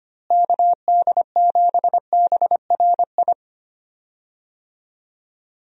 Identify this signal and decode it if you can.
Morse code, heard as a keyed tone, about 25 wpm, 700 Hz. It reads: KD7BRI
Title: USA Call Signs - 6 Characters - Repeated 3x - 25wpm